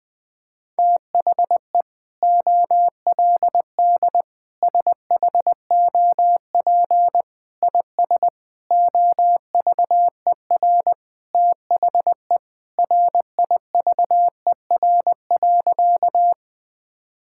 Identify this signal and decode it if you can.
Morse code, heard as a keyed tone, about 20 wpm, 700 Hz. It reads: THE OLD SHOP IS OVER THE RIVER.
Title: Sentences from Top 1600 Words 20wpm